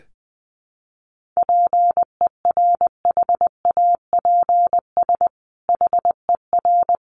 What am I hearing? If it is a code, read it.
Morse code, heard as a keyed tone, about 19 wpm, 700 Hz. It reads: PERHAPS HER
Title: Sentences from Top 1500 Words 20wpm